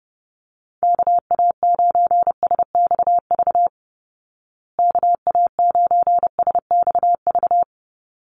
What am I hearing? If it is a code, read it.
Morse code, heard as a keyed tone, about 28 wpm, 700 Hz. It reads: KA9SXV KA9SXV
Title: Call Signs Repeated 3x - 30wpm